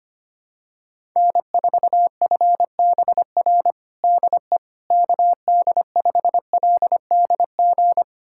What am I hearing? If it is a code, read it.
Morse code, heard as a keyed tone, about 25 wpm, 700 Hz. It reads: N4FBR DE KD5LDG
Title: USA Callsign DE Callsign 25wpm